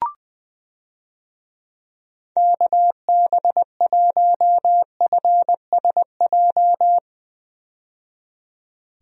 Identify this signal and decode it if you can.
Morse code, heard as a keyed tone, about 21 wpm, 700 Hz. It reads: KB1FSJ
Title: USA Call Signs - 6 Characters - 20wpm